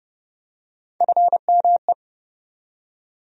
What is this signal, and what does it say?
Morse code, heard as a keyed tone, about 30 wpm, 700 Hz. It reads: FME
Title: USA Call Sign Postfixes - 3 Letters 30wpm